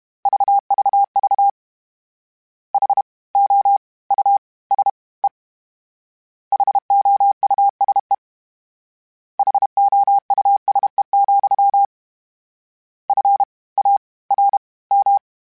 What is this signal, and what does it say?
Morse code, heard as a keyed tone, about 32 wpm, 800 Hz. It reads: VVV H O U S E HOUSE HOUSE, F A R M
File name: General / General-mp3s / Houses @30